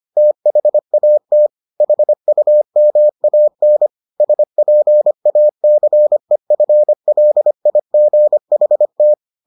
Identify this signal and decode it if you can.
Morse code, heard as a keyed tone, about 25 wpm, 600 Hz. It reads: THAT HUMAN SPACEFLIGHT